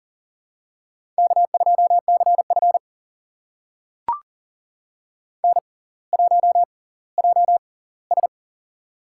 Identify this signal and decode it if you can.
Morse code, heard as a keyed tone, about 39 wpm, spaced as for 20 wpm, 700 Hz. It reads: K2CF N1JS
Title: USA Call Signs - 4 Characters - 6x Character Spacing - 40wpm